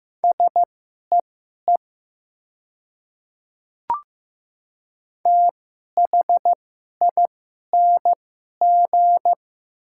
Morse code, heard as a keyed tone, about 15 wpm, spaced as for 11 wpm, 700 Hz. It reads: SEE THING
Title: 2x Character Spacing - Top 100 Words - 15wpm